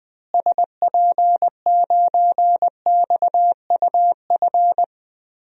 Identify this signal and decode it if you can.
Morse code, heard as a keyed tone, about 20 wpm, 700 Hz. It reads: SP9XUF
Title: DXCC-100 Call Signs - 6 Characters - Repeated 2x - 20wpm